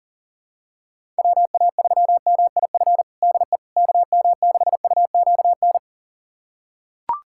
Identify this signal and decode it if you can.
Morse code, heard as a keyed tone, about 40 wpm, 700 Hz. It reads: WA3MIF DE KM6UQN
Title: USA Callsign DE Callsign - 6 Characters - 40wpm